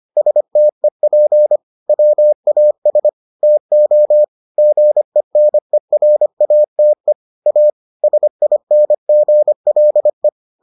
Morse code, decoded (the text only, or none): STEP WAS TO GENERATE A SINGLE